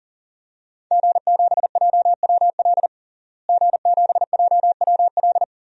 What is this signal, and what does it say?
Morse code, heard as a keyed tone, about 38 wpm, 700 Hz. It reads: G7JWL G7JWL
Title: DXCC-100 Call Signs - 5 Characters - Repeated 3x - 40wpm